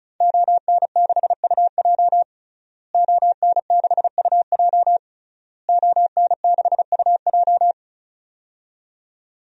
Morse code, heard as a keyed tone, about 34 wpm, 700 Hz. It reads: ON6UJ ON6UJ ON6UJ
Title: DXCC-100 Call Signs - 5 Characters - Repeated 3x - 35wpm